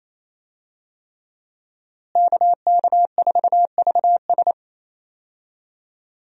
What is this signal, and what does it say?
Morse code, heard as a keyed tone, about 28 wpm, 700 Hz. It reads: KK4VS
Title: USA Call Signs - 5 Characters - Repeated 3x - 28wpm